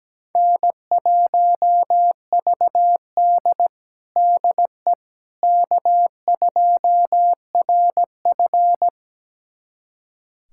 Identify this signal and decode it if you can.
Morse code, heard as a keyed tone, about 17 wpm, 700 Hz. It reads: N1VD DE K2RF
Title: USA Callsign DE Callsign - 4 Characters - 17wpm